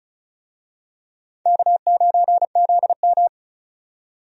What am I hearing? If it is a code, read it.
Morse code, heard as a keyed tone, about 36 wpm, 700 Hz. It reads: K9ZM
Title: USA Call Signs - 4 Characters - 35wpm